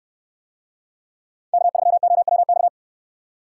Morse code, heard as a keyed tone, about 69 wpm, 700 Hz. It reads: K3OWX